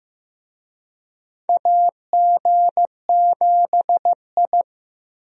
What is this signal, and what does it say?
Morse code, heard as a keyed tone, about 15 wpm, 700 Hz. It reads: AG7I
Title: USA Call Signs - 4 Characters - Repeated 3x - 15wpm